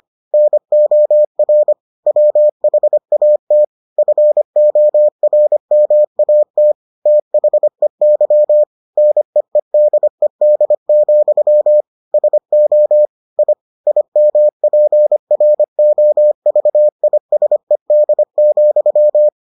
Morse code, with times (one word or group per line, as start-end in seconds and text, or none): 0.3-1.7 NOR
2.1-3.6 WHAT
4.0-6.7 FORMAT
7.1-8.6 THEY
9.0-11.8 NEEDED,
12.1-13.1 SO
13.4-13.5 I
13.9-19.3 IMPROVISED,